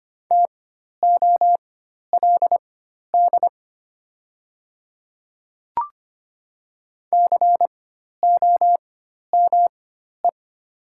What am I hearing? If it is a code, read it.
Morse code, heard as a keyed tone, about 25 wpm, spaced as for 12 wpm, 700 Hz. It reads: TOLD COME
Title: 4x Character Spacing - Top 500 Words - 25wpm